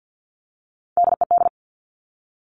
Morse code, heard as a keyed tone, about 50 wpm, 700 Hz. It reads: DED
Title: Top Three Letter Combinations in Words 50wpm